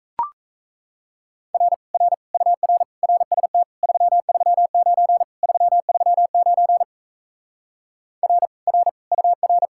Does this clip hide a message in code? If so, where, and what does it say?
Yes: Morse code from 1.5 to 9.7 s, heard as a keyed tone, about 37 wpm, 700 Hz. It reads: R R UR RST 339 339 R R UR